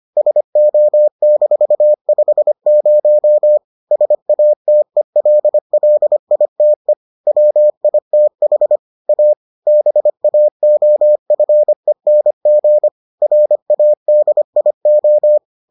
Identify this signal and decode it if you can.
Morse code, heard as a keyed tone, about 25 wpm, 600 Hz. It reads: SO-50 SATELLITE WITH A BAOFENG RADIO